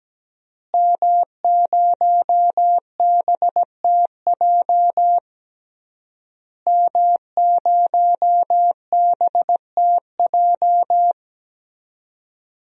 Morse code, heard as a keyed tone, about 17 wpm, 700 Hz. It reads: M0BTJ M0BTJ
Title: DXCC-100 Call Signs - 5 Characters - Repeated 3x - 17wpm